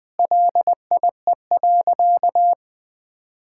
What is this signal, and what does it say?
Morse code, heard as a keyed tone, about 19 wpm, 700 Hz. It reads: LIE.
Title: Sentences from Top 1600 Words 20wpm